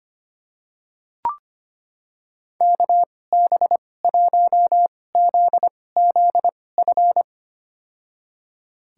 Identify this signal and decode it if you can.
Morse code, heard as a keyed tone, about 25 wpm, spaced as for 18 wpm, 700 Hz. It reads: KB1ZZF
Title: USA Call Signs - 6 Characters - 2x Character Spacing - 25wpm